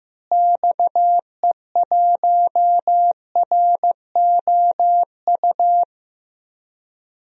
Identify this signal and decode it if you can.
Morse code, heard as a keyed tone, about 14 wpm, 700 Hz. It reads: XE1ROU
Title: DXCC-100 Call Signs - 6 Characters - Repeated 3x - 15wpm